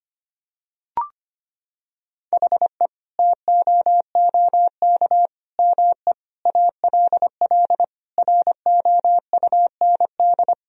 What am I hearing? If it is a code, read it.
Morse code, heard as a keyed tone, about 25 wpm, 700 Hz. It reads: HE TOOK ME ALL ROUND
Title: Sentences from Top 1000 Words - Review Part 1 - 25wpm